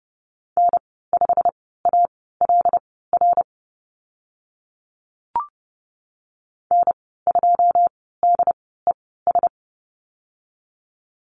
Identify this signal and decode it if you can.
Morse code, heard as a keyed tone, about 31 wpm, spaced as for 17 wpm, 700 Hz. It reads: N5ALR N2DES
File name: Lesson 15g - Number 5 - Call Signs 30wpm